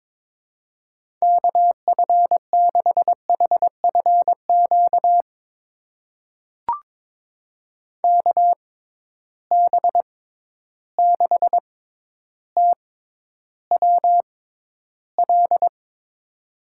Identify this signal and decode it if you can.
Morse code, heard as a keyed tone, about 22 wpm, spaced as for 11 wpm, 700 Hz. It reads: KF6HFQ KB6TWL
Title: USA Call Signs - 6 Characters - 6x Character Spacing - 22wpm